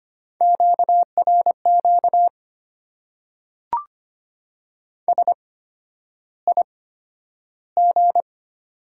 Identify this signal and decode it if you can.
Morse code, heard as a keyed tone, about 25 wpm, spaced as for 11 wpm, 690 Hz. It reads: QRQ SIG
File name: Top 90 QSO Elements - 8x Character Spacing - 25wpm